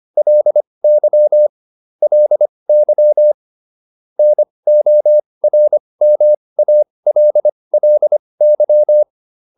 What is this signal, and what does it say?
Morse code, heard as a keyed tone, about 25 wpm, spaced as for 20 wpm, 600 Hz. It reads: LY LY NORMALLY